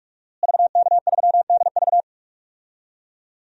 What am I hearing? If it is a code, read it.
Morse code, heard as a keyed tone, about 47 wpm, 700 Hz. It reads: VK3DV
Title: DXCC-100 Call Signs - 5 Characters - Repeated 2x - 45wpm